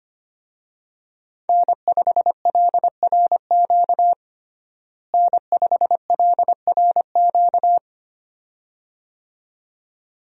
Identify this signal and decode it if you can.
Morse code, heard as a keyed tone, about 25 wpm, 700 Hz. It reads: N5LRQ N5LRQ
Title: USA Call Signs - 5 Characters - Repeated 2x - 25wpm